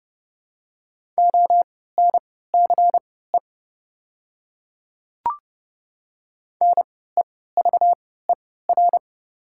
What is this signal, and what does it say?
Morse code, heard as a keyed tone, about 30 wpm, spaced as for 17 wpm, 700 Hz. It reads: ONCE NEVER
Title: 3x Character Spacing - Top 300 Words - 30wpm